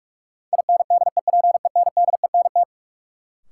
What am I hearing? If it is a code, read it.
Morse code, heard as a keyed tone, about 47 wpm, 700 Hz. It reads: INDEPENDENT
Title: Sentences from Top 900 Words - New Word Drill - 45wpm